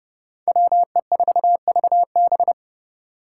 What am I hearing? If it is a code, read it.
Morse code, heard as a keyed tone, about 27 wpm, 700 Hz. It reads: WE4VB